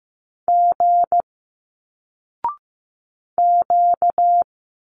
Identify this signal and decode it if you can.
Morse code, heard as a keyed tone, about 15 wpm, 700 Hz. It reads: G Q
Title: Single Letters 15wpm